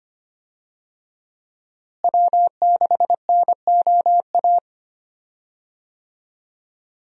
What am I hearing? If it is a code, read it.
Morse code, heard as a keyed tone, about 25 wpm, 700 Hz. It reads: W6NOA
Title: USA Call Signs - 5 Characters - Repeated 2x - 25wpm